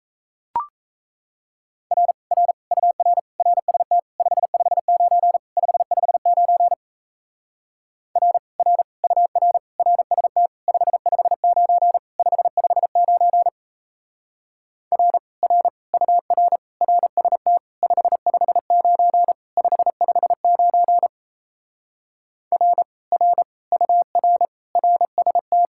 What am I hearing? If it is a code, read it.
Morse code, heard as a keyed tone, about 37 wpm, 700 Hz. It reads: R R UR RST 559 559 R R UR RST 559 559 R R UR RST 559 559 R R UR RST